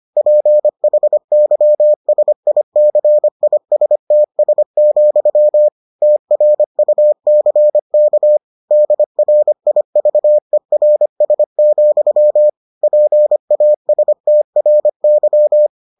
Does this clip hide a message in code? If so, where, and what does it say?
Yes: Morse code from 0.2 to 15.7 s, heard as a keyed tone, about 25 wpm, 600 Hz. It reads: PHYSICISTS, TRUCK DRIVERS, PASTRY